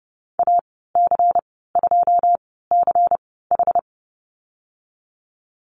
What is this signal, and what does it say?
Morse code, heard as a keyed tone, about 30 wpm, spaced as for 17 wpm, 700 Hz. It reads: AC2CH
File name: Lesson 19g - Letter w - Call Signs 30wpm